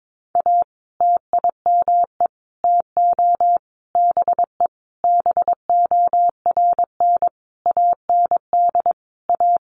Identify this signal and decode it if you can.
Morse code, heard as a keyed tone, about 22 wpm, 700 Hz. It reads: A TIME TO BE BORN AND A
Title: Sentences from Top 1300 Words 22wpm